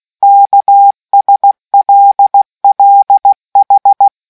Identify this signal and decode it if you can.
Morse code, heard as a keyed tone, about 16 wpm, 800 Hz. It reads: KSLLH